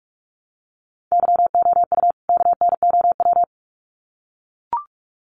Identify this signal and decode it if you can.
Morse code, heard as a keyed tone, about 44 wpm, 700 Hz. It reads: YOU KNOW
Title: Sets of 2 Words 45wpm